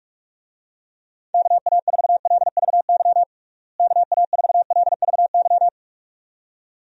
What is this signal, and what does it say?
Morse code, heard as a keyed tone, about 47 wpm, 700 Hz. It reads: KA4LVY KA4LVY
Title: USA Call Signs - 6 Characters - Repeated 2x - 45wpm